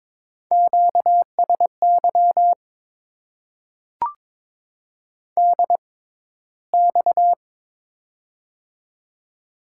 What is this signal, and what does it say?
Morse code, heard as a keyed tone, about 23 wpm, spaced as for 14 wpm, 700 Hz. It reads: QSY DX
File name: Top 90 QSO Elements - 6x Character Spacing - 22wpm